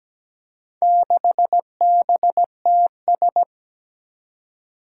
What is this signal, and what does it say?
Morse code, heard as a keyed tone, about 17 wpm, 700 Hz. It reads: 6BTS
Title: Four Challenging Letter-Number Combinations 17wpm